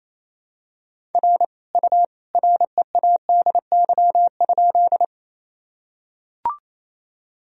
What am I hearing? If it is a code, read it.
Morse code, heard as a keyed tone, about 28 wpm, 700 Hz. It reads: R U READY?
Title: Sentences from Top 400 Words and Abbreviations 28wpm